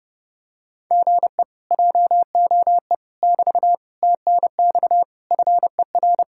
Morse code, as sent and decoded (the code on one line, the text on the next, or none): --. / .--- --- . / -...- / - -. -..- / ..-. .-.
GE JOE <BT> TNX FER